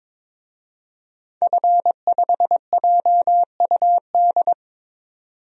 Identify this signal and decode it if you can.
Morse code, heard as a keyed tone, about 22 wpm, 700 Hz. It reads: F5JUD
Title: DXCC-100 Call Signs - 5 Characters - 22wpm